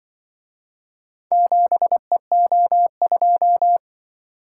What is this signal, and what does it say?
Morse code, heard as a keyed tone, about 24 wpm, 700 Hz. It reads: 7EO2
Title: Speed-Racing - Four Letter-Number Combinations - 20wpm